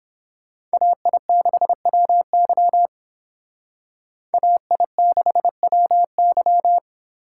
Morse code, heard as a keyed tone, about 30 wpm, 700 Hz. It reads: AI6WY AI6WY